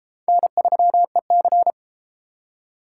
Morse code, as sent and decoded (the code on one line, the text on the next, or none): -. ...-- . -.-.
N3EC